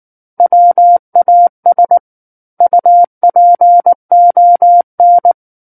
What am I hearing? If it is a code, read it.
Morse code, heard as a keyed tone, about 19 wpm, 700 Hz. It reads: WAS UPON
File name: Book One / 10-Chapter 10 - 19 WPM